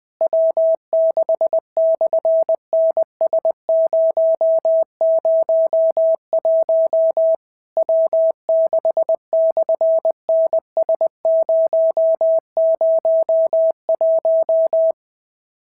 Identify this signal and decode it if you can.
Morse code, heard as a keyed tone, about 19 wpm, 650 Hz. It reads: W6/NS001 W6/NS001
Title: SOTA QSO Practice 20wpm